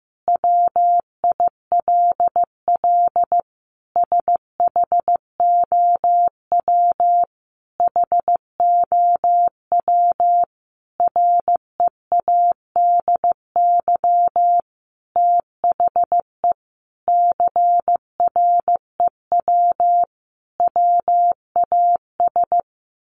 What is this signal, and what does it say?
Morse code, heard as a keyed tone, about 15 wpm, 700 Hz. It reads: WILL SHOW HOW READY THE CREW WAS
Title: Sentences from Top 1500 Words 15wpm